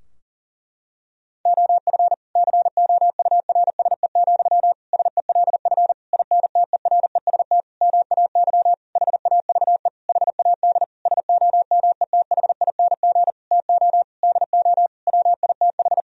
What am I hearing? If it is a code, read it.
Morse code, heard as a keyed tone, about 40 wpm, 700 Hz. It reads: OF COURSE, SELF INTEREST MAY HAVE HAD SOMETHING TO DO WITH
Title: Sentences from Top 1000 Words - Review Part 2 - 40wpm